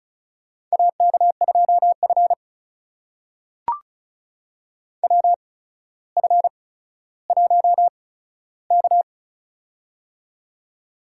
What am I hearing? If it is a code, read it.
Morse code, heard as a keyed tone, about 36 wpm, spaced as for 15 wpm, 700 Hz. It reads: AK2F WF1K
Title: USA Call Signs - 4 Characters - 8x Character Spacing - 35wpm